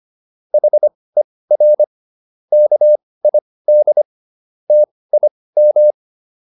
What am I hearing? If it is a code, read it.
Morse code, heard as a keyed tone, about 25 wpm, spaced as for 18 wpm, 600 Hz. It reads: HER KID TIM